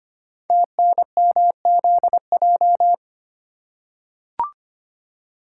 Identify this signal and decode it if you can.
Morse code, heard as a keyed tone, about 25 wpm, 700 Hz. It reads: TNMZJ
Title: Code Groups - 5 Letters - 25wpm